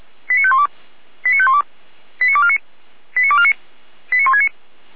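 Five-tone selective call (ZVEI), heard as quick runs of five tones, tones of about 80 ms. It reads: 89512 89512 89249 89249 89169
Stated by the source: ZVEI five-tone call received by SDR, from feuerwehr-orlamuende.org fire brigade audio archive.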